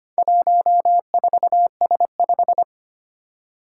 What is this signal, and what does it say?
Morse code, heard as a keyed tone, about 25 wpm, 700 Hz. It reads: 14S5
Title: Four Challenging Letter-Number Combinations 25wpm